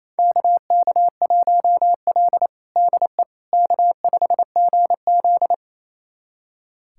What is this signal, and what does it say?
Morse code, heard as a keyed tone, about 28 wpm, 700 Hz. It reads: KK1L DE K5GZ
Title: USA Callsign DE Callsign - 4 Characters - 28wpm